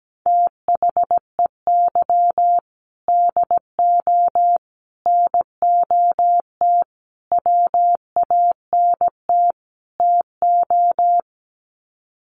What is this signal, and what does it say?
Morse code, heard as a keyed tone, about 17 wpm, 700 Hz. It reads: THEY DO NOT WANT TO
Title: Sets of 5 Words 17wpm